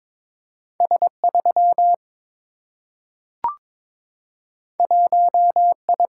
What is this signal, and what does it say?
Morse code, heard as a keyed tone, about 22 wpm, 700 Hz. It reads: S3 1I
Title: Single-Letter Single-Number 22wpm